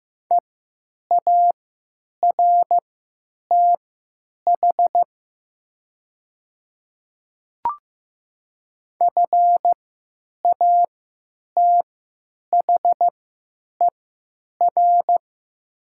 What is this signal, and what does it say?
Morse code, heard as a keyed tone, about 15 wpm, spaced as for 9 wpm, 700 Hz. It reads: EARTH FATHER
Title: 3x Character Spacing - Top 200 Words - 15wpm